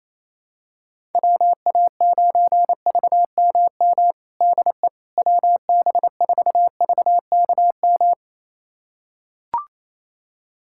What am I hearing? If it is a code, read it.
Morse code, heard as a keyed tone, about 28 wpm, 700 Hz. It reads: WA9VMM DE WB4VKM
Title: USA Callsign DE Callsign - 6 Characters - 28wpm